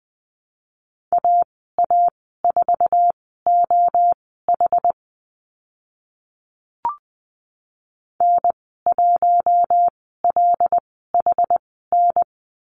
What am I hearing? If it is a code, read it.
Morse code, heard as a keyed tone, about 20 wpm, spaced as for 15 wpm, 700 Hz. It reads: AA4OH N1LHN
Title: 082 Lesson 14g - Number 2 - Call Signs 20wpm